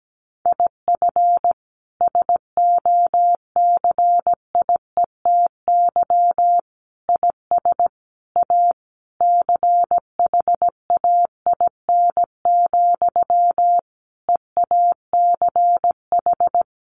Morse code, heard as a keyed tone, about 17 wpm, 700 Hz. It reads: IF SOCIETY IS A CHAIN, EACH
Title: Sentences from Top 1500 Words 17wpm